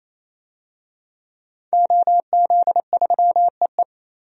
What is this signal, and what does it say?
Morse code, heard as a keyed tone, about 28 wpm, 700 Hz. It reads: OZ3EE
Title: DXCC-100 Call Signs - 5 Characters - 28wpm